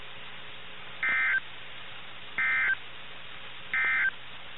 Emergency Alert System (SAME) tones, heard as harsh buzzing bursts. It is heard from 0.9 to 4.2 s.
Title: NOAA Weather Radio - KIH61